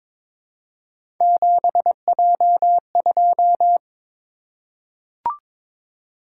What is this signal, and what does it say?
Morse code, heard as a keyed tone, about 22 wpm, 700 Hz. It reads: 7J2
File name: practice / 22wpm / Three Letter-Number Combinations - 22wpm